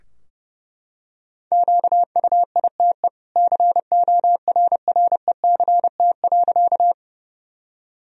Morse code, heard as a keyed tone, about 32 wpm, 700 Hz. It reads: QUITE CORRECT.
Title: Sentences from Top 1000 Words - Review Part 2 - 30wpm